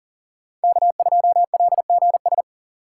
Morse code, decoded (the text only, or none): K2LGS